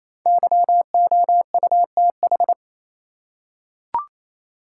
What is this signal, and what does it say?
Morse code, heard as a keyed tone, about 28 wpm, 700 Hz. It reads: YOUTH
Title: Five Letter Words - 28wpm